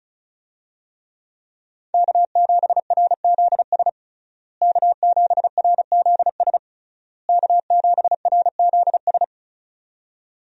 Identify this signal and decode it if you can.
Morse code, heard as a keyed tone, about 34 wpm, 700 Hz. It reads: K7RZS K7RZS K7RZS
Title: USA Call Signs - 5 Characters - Repeated 3x - 35wpm